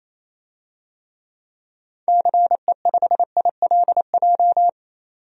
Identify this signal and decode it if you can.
Morse code, heard as a keyed tone, about 28 wpm, 700 Hz. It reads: CE5ILJ
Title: DXCC-100 Call Signs - 6 Characters - 28wpm